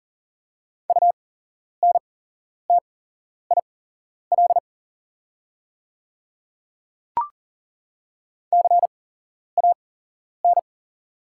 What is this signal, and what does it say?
Morse code, heard as a keyed tone, about 39 wpm, spaced as for 11 wpm, 690 Hz. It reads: UNTIL CAN
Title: 8x Character Spacing - Top 500 Words - 40wpm